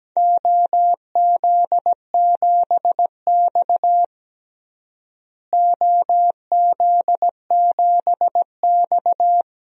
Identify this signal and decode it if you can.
Morse code, heard as a keyed tone, about 17 wpm, 700 Hz. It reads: OZ7X OZ7X
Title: DXCC-100 Call Signs - 4 Characters - Repeated 3x - 17wpm